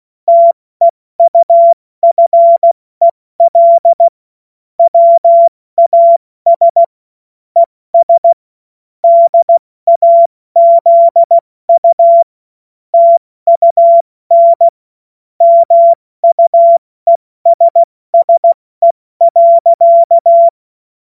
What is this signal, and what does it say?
Morse code, heard as a keyed tone, about 16 wpm, 680 Hz. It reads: TEUFEL WAS ES DAZU TUN MUESSE.
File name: Morsekurs / Witze / Witz 28.November 2023 70 F80